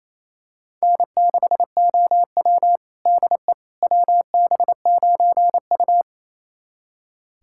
Morse code, heard as a keyed tone, about 28 wpm, 700 Hz. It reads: N6OW DE WB9U
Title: USA Callsign DE Callsign - 4 Characters - 28wpm